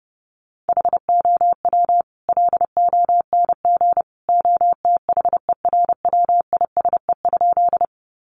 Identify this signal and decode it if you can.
Morse code, heard as a keyed tone, about 30 wpm, 700 Hz. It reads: HOW LONG OTHERWISE?
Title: Sentences from Top 1200 Words 30wpm